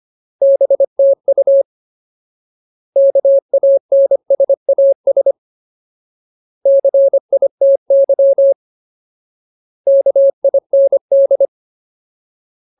Morse code, heard as a keyed tone, about 25 wpm, 550 Hz. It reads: BTU KANSAS CITY KIND
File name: Ultimate Word List / 25WPM Ultimate